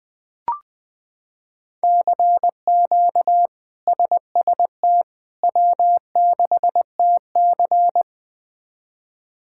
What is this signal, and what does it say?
Morse code, heard as a keyed tone, about 20 wpm, 700 Hz. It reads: CQ SST W6TC
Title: K1USN SST QSO Practice - 20wpm